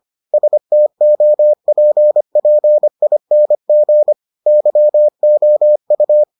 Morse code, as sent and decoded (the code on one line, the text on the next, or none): ... - --- .--. .--. .. -. --. / -.-- --- ..-
STOPPING YOU